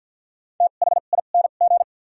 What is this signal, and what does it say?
Morse code, heard as a keyed tone, about 52 wpm, 700 Hz. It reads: T H I N G
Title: 2x Character Spacing - Top 200 Words - 50wpm